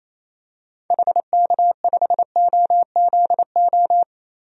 Morse code, heard as a keyed tone, about 28 wpm, 700 Hz. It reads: HK5OZO
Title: DXCC-100 Call Signs - 6 Characters - 28wpm